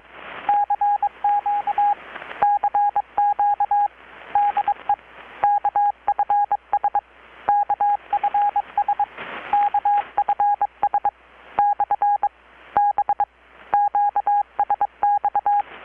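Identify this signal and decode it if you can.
Morse code, heard as a keyed tone, about 22 wpm, spaced as for 19 wpm, 820 Hz, with radio noise. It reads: CQ CQ DE KFS KFS KFS/B QSX
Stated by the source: Morse from KPH's Night of Nights XX maritime radiotelegraphy event, received in Chiapas, Mexico about 3,800 km away on a Kenwood R-600 with a grounded 10 m long-wire antenna.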